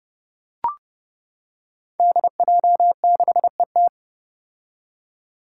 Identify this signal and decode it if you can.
Morse code, heard as a keyed tone, about 30 wpm, 700 Hz. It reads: DJ6ET
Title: Five Challenging Letter-Number Combinations 30wpm